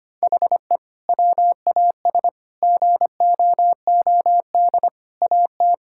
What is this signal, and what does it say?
Morse code, heard as a keyed tone, about 25 wpm, 700 Hz. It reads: HE WAS GOOD AT
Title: Sentences from Top 1000 Words - Review Part 5 - 25wpm